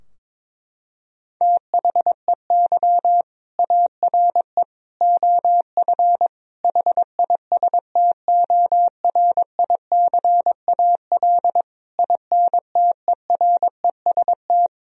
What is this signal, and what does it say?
Morse code, heard as a keyed tone, about 22 wpm, 700 Hz. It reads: THEY ARE OF HISTORICAL INTEREST
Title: Sentences from Top 1000 Words - Review Part 4 - 22wpm